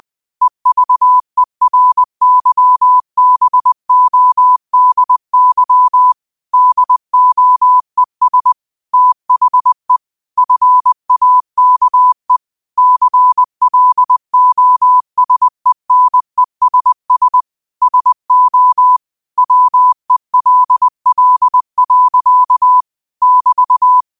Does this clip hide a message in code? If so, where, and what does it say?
Yes: Morse code from 0.4 to 24.0 s, heard as a keyed tone, about 20 wpm, 1000 Hz. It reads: EVERYBODY DOES THE FAKE CLOSENESS SO WELL. <BT>